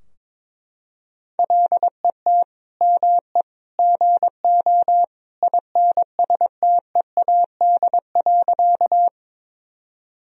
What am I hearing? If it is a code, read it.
Morse code, heard as a keyed tone, about 22 wpm, 700 Hz. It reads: LET ME GO INSTEAD.
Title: Sentences from Top 1000 Words - Review Part 4 - 22wpm